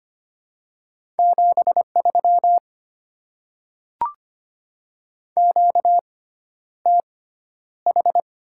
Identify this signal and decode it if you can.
Morse code, heard as a keyed tone, about 25 wpm, spaced as for 11 wpm, 700 Hz. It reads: 73 QTH
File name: Top 20 QSO Elements - 6x Character Spacing - 25wpm